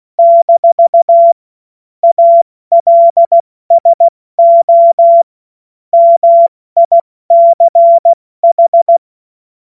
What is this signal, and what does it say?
Morse code, heard as a keyed tone, about 16 wpm, 680 Hz. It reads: - ALSO MICH
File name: Morsekurs/Witze/Witz 12 Dezember 2023 70 F80